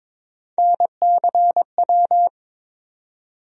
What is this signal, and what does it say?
Morse code, heard as a keyed tone, about 22 wpm, 700 Hz. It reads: NCW
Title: USA Call Sign Postfixes - 3 Letters 22wpm